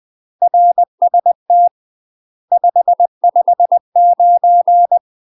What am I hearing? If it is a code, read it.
Morse code, heard as a keyed tone, about 20 wpm, 700 Hz. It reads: RST 559
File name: UK MASTER QSOs TRAINING/20WPM